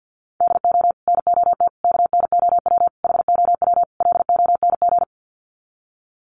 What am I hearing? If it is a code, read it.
Morse code, heard as a keyed tone, about 50 wpm, 700 Hz. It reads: DO NOT KNOW HOW LONG
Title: Sets of 5 Words Encore2 50wpm